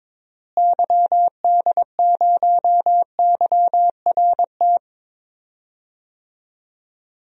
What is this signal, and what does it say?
Morse code, heard as a keyed tone, about 22 wpm, 700 Hz. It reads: YD0YRT
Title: DXCC-100 Call Signs - 6 Characters - Repeated 2x - 22wpm